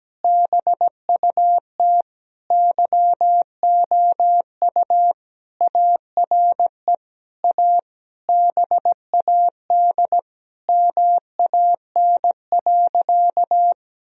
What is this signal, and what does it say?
Morse code, heard as a keyed tone, about 17 wpm, 700 Hz. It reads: BUT YOU ARE A BAD MAN.
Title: Sentences from Top 1000 Words - Review Part 1 - 17wpm